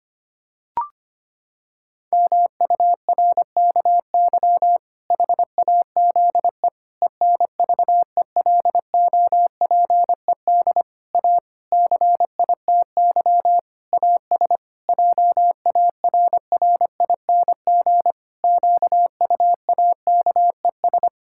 Morse code, heard as a keyed tone, about 25 wpm, 700 Hz. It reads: MURKY HAZE ENVELOPED A CITY AS JARRING QUAKES
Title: Pangram Challenge v2 - 25wpm